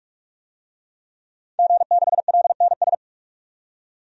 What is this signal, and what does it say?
Morse code, heard as a keyed tone, about 46 wpm, 700 Hz. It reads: G6LNS
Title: DXCC-100 Call Signs - 5 Characters - 45wpm